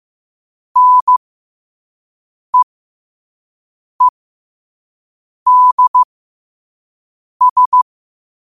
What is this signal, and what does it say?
Morse code, heard as a keyed tone, about 15 wpm, spaced as for 5 wpm, 1000 Hz. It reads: NEEDS
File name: qotdfiles / 5wpm